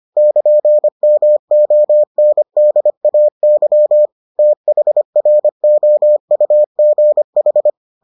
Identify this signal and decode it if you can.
Morse code, heard as a keyed tone, about 25 wpm, 600 Hz. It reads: <KN>MONDAY THROUGH